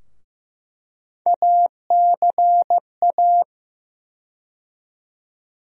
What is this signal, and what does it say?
Morse code, heard as a keyed tone, about 15 wpm, 700 Hz. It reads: ACA
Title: USA Call Sign Postfixes - 3 Letters 15wpm